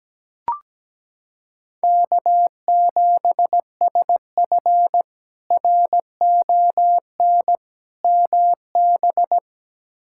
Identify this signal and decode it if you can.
Morse code, heard as a keyed tone, about 17 wpm, 700 Hz. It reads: K7SF RON MB
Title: K1USN SST QSO Practice - 17wpm